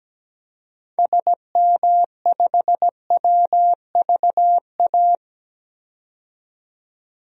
Morse code, heard as a keyed tone, about 17 wpm, 700 Hz. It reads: SM5WVA
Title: DXCC-100 Call Signs - 6 Characters - 17wpm